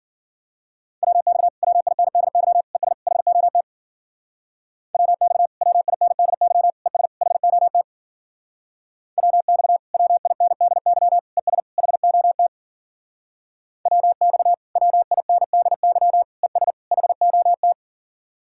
Morse code, heard as a keyed tone, about 52 wpm, 700 Hz. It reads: WX WINDY ES HOT WX WINDY ES HOT WX WINDY ES HOT WX WINDY ES HOT